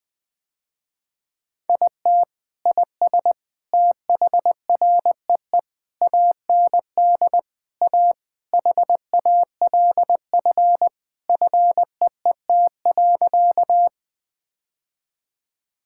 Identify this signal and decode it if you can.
Morse code, heard as a keyed tone, about 21 wpm, 700 Hz. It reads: IT IS THREE AND A HALF FEET.
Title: Sentences with 3-Word Phrases - Spoken Beginning - 20wpm